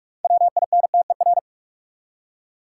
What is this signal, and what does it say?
Morse code, heard as a keyed tone, about 42 wpm, 700 Hz. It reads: WINTER
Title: Six Letter Words - 45wpm